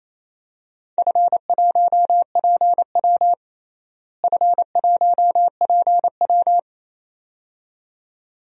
Morse code, heard as a keyed tone, about 28 wpm, 700 Hz. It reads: F1PW F1PW
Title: DXCC-100 Call Signs - 4 Characters - Repeated 2x - 28wpm